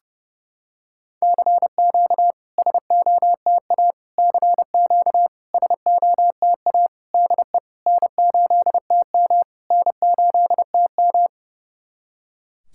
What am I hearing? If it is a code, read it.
Morse code, heard as a keyed tone, about 30 wpm, 690 Hz. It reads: CQ SOTA CQ SOTA DE N8TM N8TM
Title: SOTA QSO Practice 30wpm